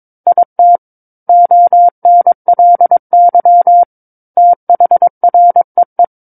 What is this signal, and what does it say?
Morse code, heard as a keyed tone, about 22 wpm, 700 Hz. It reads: IT ONLY THREE
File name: Book One / 13-Chapter 13 - 22 WPM